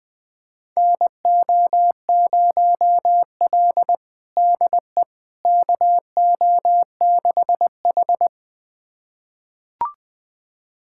Morse code, heard as a keyed tone, about 20 wpm, 700 Hz. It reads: NO0L DE KO6H
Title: USA Callsign DE Callsign - 4 Characters - 20wpm